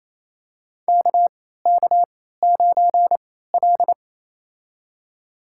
Morse code, heard as a keyed tone, about 28 wpm, spaced as for 16 wpm, 700 Hz. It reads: KK9L
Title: USA Call Signs - 4 Characters - 3x Character Spacing - 28wpm